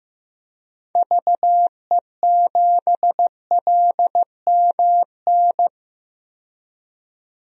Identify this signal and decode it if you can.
Morse code, heard as a keyed tone, about 15 wpm, 700 Hz. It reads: VE7LMN
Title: DXCC-100 Call Signs - 6 Characters - 15wpm